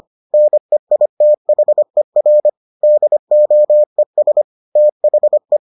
Morse code, heard as a keyed tone, about 25 wpm, 600 Hz. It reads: NEITHER DOES THE